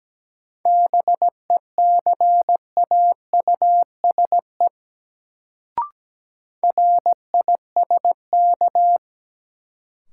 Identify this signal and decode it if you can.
Morse code, heard as a keyed tone, about 17 wpm, 700 Hz. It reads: BECAUSE RISK
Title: Sentences from Top 600 Words - New Word Drill - 17wpm